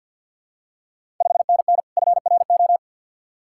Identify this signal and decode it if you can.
Morse code, heard as a keyed tone, about 52 wpm, 700 Hz. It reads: 5NN FRO
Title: Ontario QSO Party Exchange 50wpm